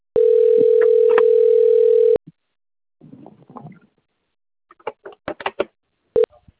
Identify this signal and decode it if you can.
Ringback tone (North America), 440 + 480 Hz.